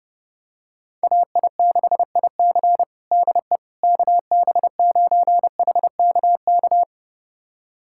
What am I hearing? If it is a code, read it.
Morse code, heard as a keyed tone, about 30 wpm, 700 Hz. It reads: AI6IC DE KB9HKK
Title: USA Callsign DE Callsign 30wpm